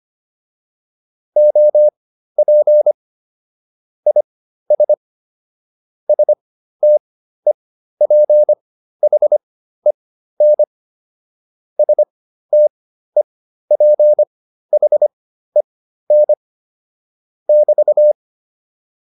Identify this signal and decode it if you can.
Morse code, heard as a keyed tone, about 25 wpm, spaced as for 13 wpm, 600 Hz. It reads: OP IS STEPHEN STEPHEN <BT>